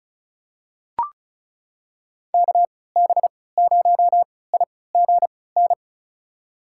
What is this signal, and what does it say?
Morse code, heard as a keyed tone, about 36 wpm, spaced as for 20 wpm, 700 Hz. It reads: KB0IGN
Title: USA Call Signs - 3x Character Spacing - 35wpm